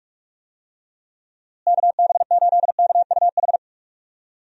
Morse code, heard as a keyed tone, about 46 wpm, 700 Hz. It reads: KB8KUH
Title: USA Call Signs - 6 Characters - Repeated 2x - 45wpm